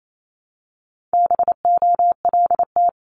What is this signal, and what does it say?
Morse code, heard as a keyed tone, about 28 wpm, 700 Hz. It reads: BOLT
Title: Multi-letter Phonemes 28wpm